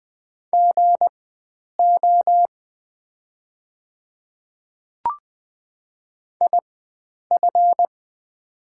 Morse code, heard as a keyed tone, about 20 wpm, spaced as for 9 wpm, 700 Hz. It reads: GO IF
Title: 4x Character Spacing - Top 100 Words - 20wpm